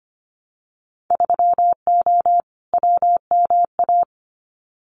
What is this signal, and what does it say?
Morse code, heard as a keyed tone, about 25 wpm, 700 Hz. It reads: 3O WMA
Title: Winter Field Day Exchanges Encore 25wpm